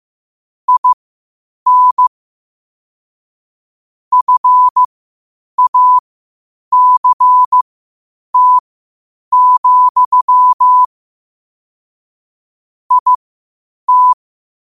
Morse code, heard as a keyed tone, about 15 wpm, spaced as for 8 wpm, 1000 Hz. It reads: IN FACT, IT
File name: qotdfiles / 8wpm